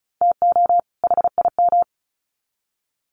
Morse code, heard as a keyed tone, about 34 wpm, 700 Hz. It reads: TO HIM
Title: Sets of 2 Words 35wpm